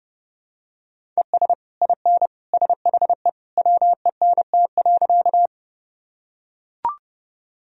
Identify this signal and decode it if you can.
Morse code, heard as a keyed tone, about 32 wpm, 700 Hz. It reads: ES IN SHE WENT.
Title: Sentences from Top 200 Words and Abbreviations 30wpm